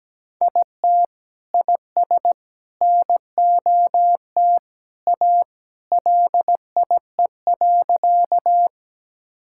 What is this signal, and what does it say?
Morse code, heard as a keyed tone, about 17 wpm, 700 Hz. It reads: IT IS NOT A LIE.
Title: Sentences from Top 1600 Words 17wpm